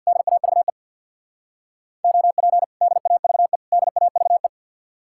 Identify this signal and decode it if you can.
Morse code, heard as a keyed tone, about 67 wpm, 700 Hz. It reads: DAVE OP DAVE DAVE